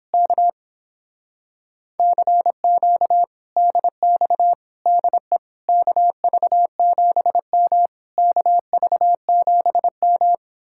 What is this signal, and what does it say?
Morse code, heard as a keyed tone, about 29 wpm, 700 Hz. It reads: K CQ DX DE KV7M KV7M